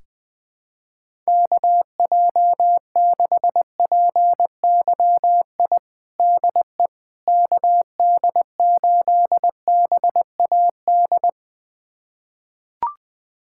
Morse code, heard as a keyed tone, about 20 wpm, 700 Hz. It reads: KJ6PYI DE KD8BAD
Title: USA Callsign DE Callsign - 6 Characters - 20wpm